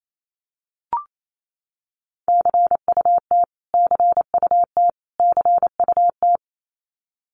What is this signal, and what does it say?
Morse code, heard as a keyed tone, about 28 wpm, 700 Hz. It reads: CUT CUT CUT
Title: Top 500 Words Repeated 28wpm